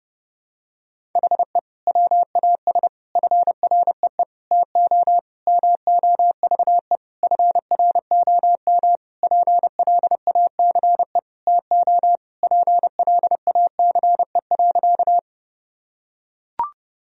Morse code, heard as a keyed tone, about 30 wpm, 700 Hz. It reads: HE WAS FREE TO MOVE FROM PLACE TO PLACE.
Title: Sentences from Top 1000 Words - Review Part 1 - 30wpm